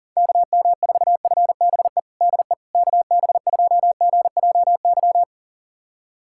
Morse code, heard as a keyed tone, about 37 wpm, 700 Hz. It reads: KM4FBE DE KB2GJY
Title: USA Callsign DE Callsign - 6 Characters - 40wpm